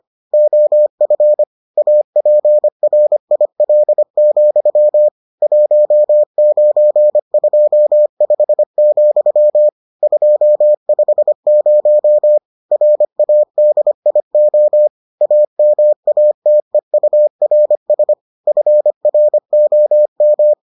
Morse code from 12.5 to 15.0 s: RADIO

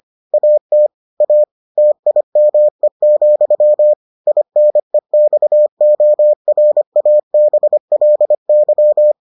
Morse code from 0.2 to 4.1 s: AT A TIME,

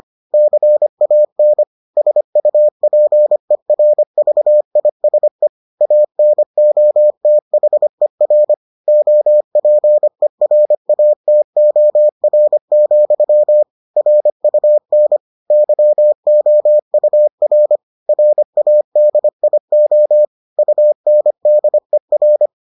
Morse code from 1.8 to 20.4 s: SUPERVISE ANOTHER OPERATOR, RUN YOUR RADIO

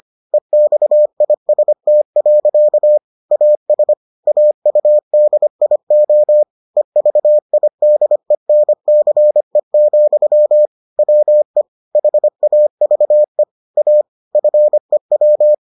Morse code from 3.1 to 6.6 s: AS AUDIO

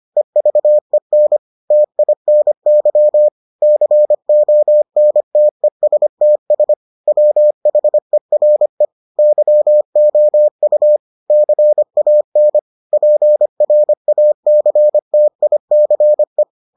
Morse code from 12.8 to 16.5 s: PRACTICE

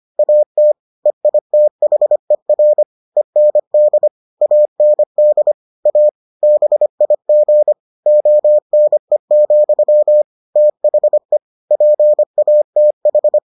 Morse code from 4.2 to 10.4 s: AND A BIG ONE,

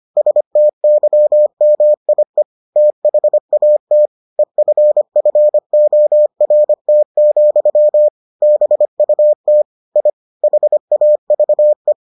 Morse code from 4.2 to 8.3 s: EFFORT,

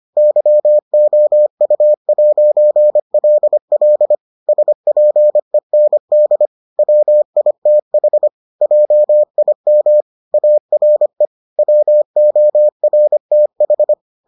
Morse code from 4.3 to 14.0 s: SPEND WITH JIM ARE WORTH

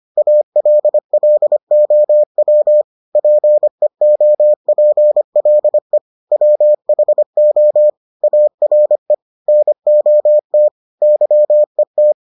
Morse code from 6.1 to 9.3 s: WHO ARE

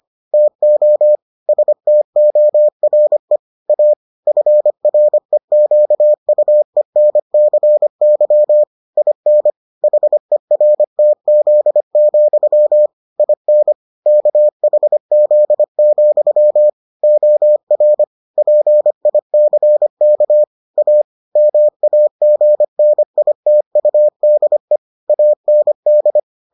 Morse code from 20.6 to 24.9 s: A MAGNITUDE

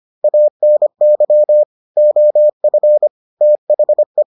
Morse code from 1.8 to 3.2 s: OF

